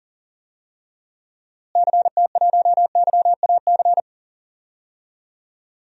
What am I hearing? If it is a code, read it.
Morse code, heard as a keyed tone, about 40 wpm, 700 Hz. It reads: CT1YAC
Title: DXCC-100 Call Signs - 6 Characters - Repeated 2x - 40wpm